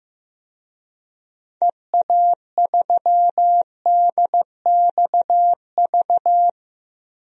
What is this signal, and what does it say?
Morse code, heard as a keyed tone, about 15 wpm, 700 Hz. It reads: EA3DXV
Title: DXCC-100 Call Signs - 6 Characters - 15wpm